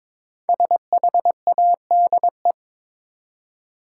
Morse code, heard as a keyed tone, about 22 wpm, 700 Hz. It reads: SHADE
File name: 5 Letter Words - First 3 Letters Given - 22wpm